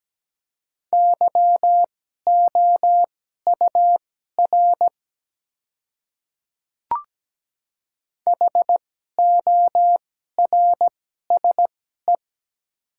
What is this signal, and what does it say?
Morse code, heard as a keyed tone, about 17 wpm, spaced as for 12 wpm, 690 Hz. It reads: YOUR HORSE
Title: 2x Character Spacing - Top 300 Words - 17wpm